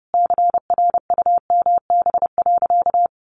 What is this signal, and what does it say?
Morse code, heard as a keyed tone, about 30 wpm, 700 Hz. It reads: CRUMB.
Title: Additional Easy Phrases 30wpm